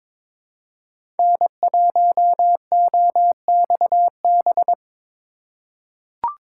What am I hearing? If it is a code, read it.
Morse code, heard as a keyed tone, about 22 wpm, 700 Hz. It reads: N1OXB
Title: USA Call Signs - 5 Characters - 3x Character Spacing - 22wpm